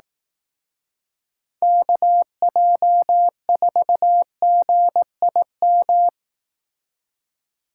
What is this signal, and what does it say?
Morse code, heard as a keyed tone, about 18 wpm, 700 Hz. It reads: KJ4GIM
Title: Speed Racing - USA Call Signs 15wpm